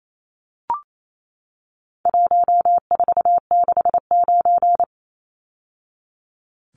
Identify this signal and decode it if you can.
Morse code, heard as a keyed tone, about 28 wpm, 700 Hz. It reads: 1469
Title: Numbers - 4-Digit 28wpm